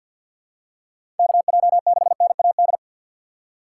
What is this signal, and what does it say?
Morse code, heard as a keyed tone, about 48 wpm, 700 Hz. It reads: KJ6NAD
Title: USA Call Signs - 6 Characters - 6x Character Spacing - 50wpm